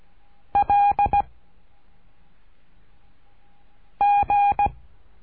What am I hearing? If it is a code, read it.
Morse code, heard as a keyed tone, about 17 wpm, 810 Hz. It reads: L G
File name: cursos / CURSO DE TELEGRAFIA / Telegrafia - Trilha 13